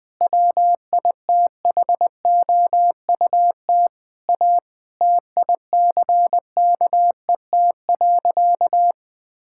Morse code, heard as a keyed tone, about 20 wpm, 700 Hz. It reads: WITHOUT A TICKET.
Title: Sentences with 6-Word Phrases - 20wpm